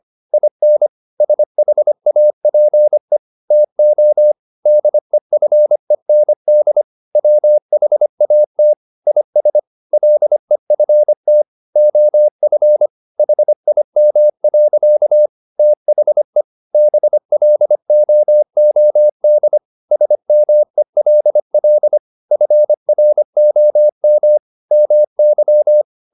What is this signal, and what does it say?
Morse code, heard as a keyed tone, about 25 wpm, 600 Hz. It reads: IN SHAPE TO DEFEND WHAT IS LEFT OF HIM. THE BLOOD SMELL FROM MY